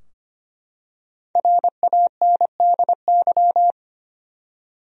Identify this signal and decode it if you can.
Morse code, heard as a keyed tone, about 25 wpm, 700 Hz. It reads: RANDY
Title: USA Common Names - Max 5 Letters - 25wpm